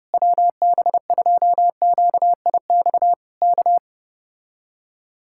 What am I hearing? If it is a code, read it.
Morse code, heard as a keyed tone, about 31 wpm, 700 Hz. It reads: WB2QIX K